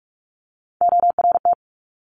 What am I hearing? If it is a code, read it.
Morse code, heard as a keyed tone, about 46 wpm, 700 Hz. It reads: ORT
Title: Top Three Letter Combinations in Words 45wpm